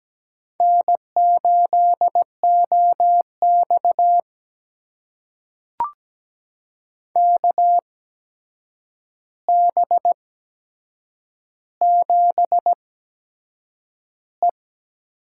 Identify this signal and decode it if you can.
Morse code, heard as a keyed tone, about 17 wpm, spaced as for 7 wpm, 700 Hz. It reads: N8OX KB7E
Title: USA Call Signs - 8x Character Spacing - 17wpm